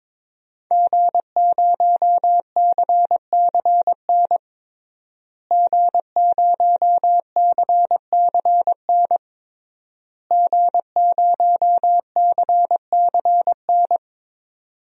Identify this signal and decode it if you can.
Morse code, heard as a keyed tone, about 22 wpm, 700 Hz. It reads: G0CCN G0CCN G0CCN
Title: DXCC-100 Call Signs - 5 Characters - Repeated 3x - 22wpm